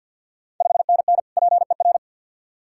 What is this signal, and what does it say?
Morse code, heard as a keyed tone, about 51 wpm, 690 Hz. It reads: 5NN PER
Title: Ontario QSO Party Exchange 50wpm